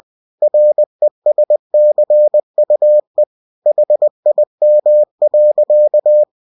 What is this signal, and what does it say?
Morse code, heard as a keyed tone, about 20 wpm, 600 Hz. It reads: RESCUE HIM.
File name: cw-books / winnie-the-pooh / winnie-20wpm-page-0101